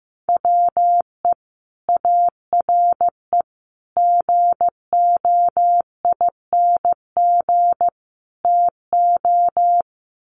Morse code, heard as a keyed tone, about 15 wpm, 700 Hz. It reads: WE ARE GOING TO
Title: Sets of 4 Words 15wpm